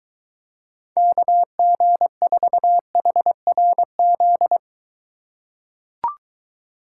Morse code, heard as a keyed tone, about 22 wpm, 700 Hz. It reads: KG4HRZ